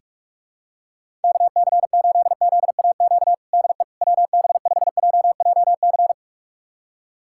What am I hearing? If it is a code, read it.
Morse code, heard as a keyed tone, about 44 wpm, 700 Hz. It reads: KC8ZAQ DE WB5JJC
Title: USA Callsign DE Callsign 45wpm